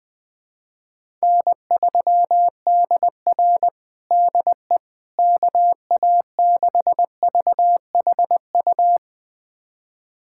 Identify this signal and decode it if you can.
Morse code, heard as a keyed tone, about 20 wpm, 700 Hz. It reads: N3DR DE KA6VHU
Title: USA Callsign DE Callsign 20wpm